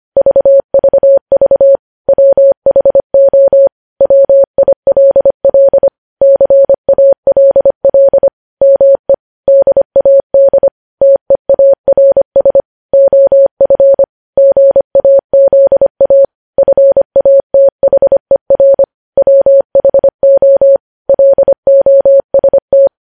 Morse code, heard as a keyed tone, about 25 wpm, 550 Hz. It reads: VVV WHO WILL CALL ME DAD TEARS OF GAZA FATHER WHO LOST
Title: News Headlines at 25